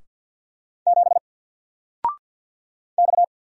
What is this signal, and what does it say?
Morse code, heard as a keyed tone, about 50 wpm, 700 Hz. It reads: 7 X
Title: Single Letter-Number 50wpm